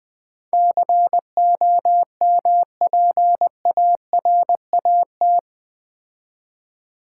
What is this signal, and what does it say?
Morse code, heard as a keyed tone, about 20 wpm, 700 Hz. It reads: COMPARAT
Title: Build A Word - 4 Letters At A Time - 11-Letter Words - 20wpm